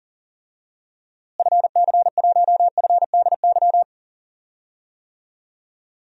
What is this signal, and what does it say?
Morse code, heard as a keyed tone, about 40 wpm, 700 Hz. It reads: FC1FDY
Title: DXCC-100 Call Signs - 6 Characters - Repeated 3x - 40wpm